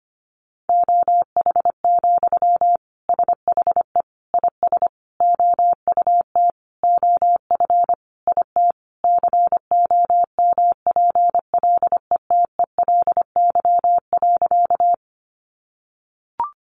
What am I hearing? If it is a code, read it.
Morse code, heard as a keyed tone, about 25 wpm, 700 Hz. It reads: OH, SHE IS OUT OF IT COMPLETELY.
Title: Sentences from Top 1100 Words 25wpm